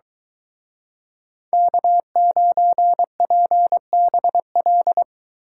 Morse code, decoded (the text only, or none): K9PBL